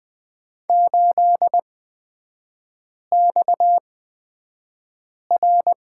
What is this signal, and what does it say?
Morse code, heard as a keyed tone, about 20 wpm, spaced as for 5 wpm, 690 Hz. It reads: 8XR
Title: 229 Lesson 37e - Letter x - Three Characters 20wpm